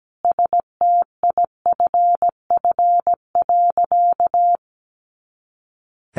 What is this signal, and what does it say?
Morse code, heard as a keyed tone, about 17 wpm, 700 Hz. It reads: STIFF.
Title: A Sound Of Thunder 17wpm